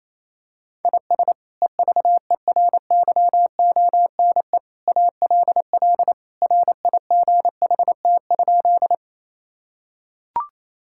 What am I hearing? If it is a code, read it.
Morse code, heard as a keyed tone, about 28 wpm, 700 Hz. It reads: IS EVERYONE ALL RIGHT?
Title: Sentences from Top 1000 Words - Review Part 5 - 28wpm